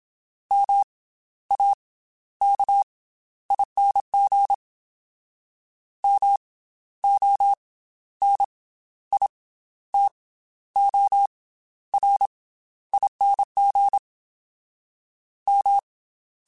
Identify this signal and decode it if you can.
Morse code, heard as a keyed tone, about 27 wpm, 800 Hz. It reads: M A K ING M O N I T O R ING M
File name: Special INGs / Special INGs-mp3s / Slow ing @25